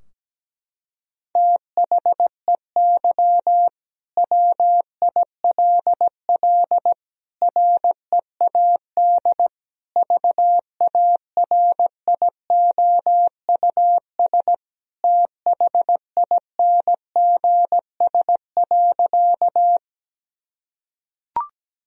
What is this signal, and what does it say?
Morse code, heard as a keyed tone, about 17 wpm, 700 Hz. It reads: THEY WILL READ VARIOUS THINGS.
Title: Sentences from Top 1000 Words - Review Part 4 - 17wpm